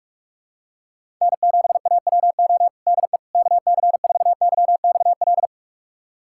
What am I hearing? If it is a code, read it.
Morse code, heard as a keyed tone, about 41 wpm, 690 Hz. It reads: N7AWO DE KC4YXL
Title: USA Callsign DE Callsign 45wpm